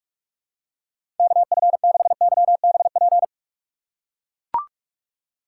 Morse code, heard as a keyed tone, about 44 wpm, 700 Hz. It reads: KF6YBP
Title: USA Call Signs - 3x Character Spacing - 45wpm